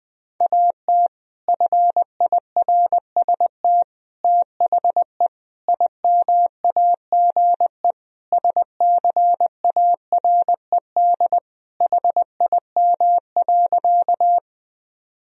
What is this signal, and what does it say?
Morse code, heard as a keyed tone, about 21 wpm, 700 Hz. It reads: AT FIRST THE IMAGE SCARED HIM.
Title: Sentences from Top 1000 Words - Review Part 4 - 20wpm